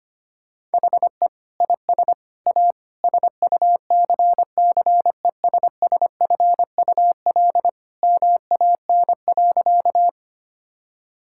Morse code, heard as a keyed tone, about 25 wpm, 700 Hz. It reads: HE IS A SUCCESSFUL MAN.
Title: Sentences from Top 1000 Words - Review Part 6 - 25wpm